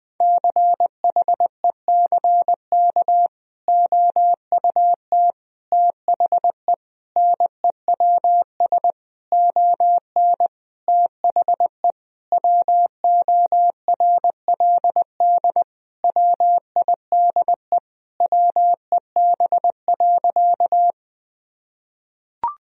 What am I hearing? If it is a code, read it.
Morse code, heard as a keyed tone, about 20 wpm, 700 Hz. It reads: CHECK OUT THE NEWS ON THE WORLD WIDE WEB.
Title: Sentences from Top 1000 Words - Review Part 4 - 20wpm